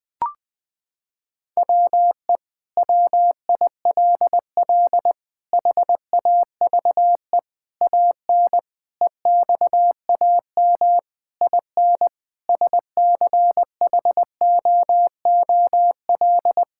Morse code, heard as a keyed tone, about 19 wpm, 700 Hz. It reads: WE WILL HAVE AN EXAM IN SCHOOL
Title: Sentences from Top 1000 Words - Review Part 5 - 20wpm